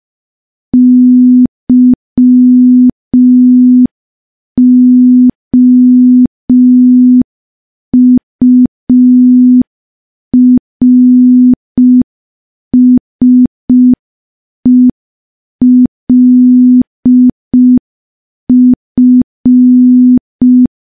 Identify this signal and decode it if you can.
Morse code, heard as a keyed tone, about 5 wpm, 250 Hz. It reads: YOURSELF
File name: morse / morse-training-05WPM